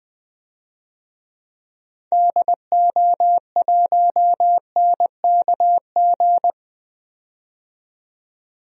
Morse code, heard as a keyed tone, about 20 wpm, 700 Hz. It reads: DO1NKG
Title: DXCC-100 Call Signs - 6 Characters - 20wpm